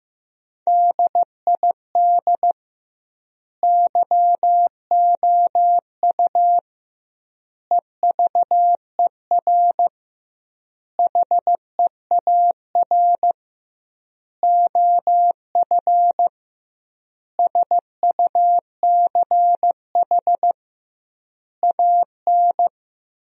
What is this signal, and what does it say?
Morse code, heard as a keyed tone, about 16 wpm, 700 Hz. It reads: DID YOU EVER HEAR OF SUCH AN
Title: 2x Word-Spacing - Sentences from Top 400 Words - 15wpm